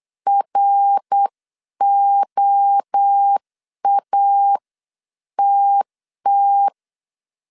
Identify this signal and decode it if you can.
Morse code, heard as a keyed tone, about 8 wpm, 800 Hz. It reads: ROATT